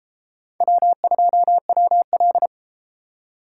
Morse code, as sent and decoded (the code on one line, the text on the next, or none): .-- ..--- .-- .-..
W2WL